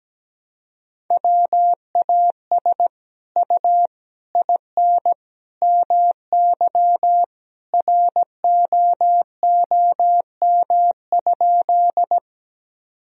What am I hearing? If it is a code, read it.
Morse code, heard as a keyed tone, about 17 wpm, 700 Hz. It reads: WAS U IN MY ROOM?
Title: Sentences from Top 300 Words and Abbreviations 17wpm